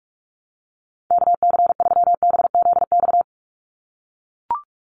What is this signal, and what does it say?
Morse code, heard as a keyed tone, about 44 wpm, 700 Hz. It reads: KC3BZX
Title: USA Call Signs 45wpm